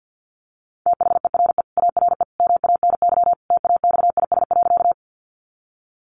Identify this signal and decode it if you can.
Morse code, heard as a keyed tone, about 48 wpm, 700 Hz. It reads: THERE ARE MANY TAXIS.
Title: Easy Phrases 50wpm